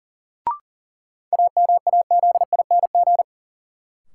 Morse code, heard as a keyed tone, about 40 wpm, 700 Hz. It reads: AMAZING
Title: Sentences from Top 1500 Words - New Word Drill - 40wpm